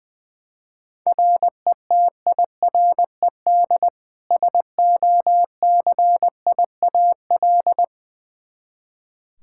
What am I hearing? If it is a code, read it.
Morse code, heard as a keyed tone, about 21 wpm, 700 Hz. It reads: RETIRED SOCIAL
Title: K1USN SST QSO Practice 20wpm